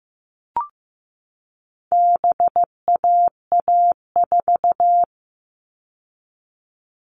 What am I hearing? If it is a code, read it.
Morse code, heard as a keyed tone, about 15 wpm, 700 Hz. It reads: BAA4
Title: All Cut Numbers - 4-Digit 15wpm